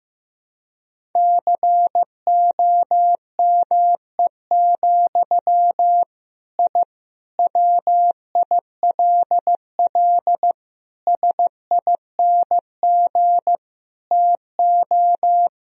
Morse code, heard as a keyed tone, about 15 wpm, 700 Hz. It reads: COME, I WILL SING TO
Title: Sentences from Top 1000 Words - Review Part 2 - 15wpm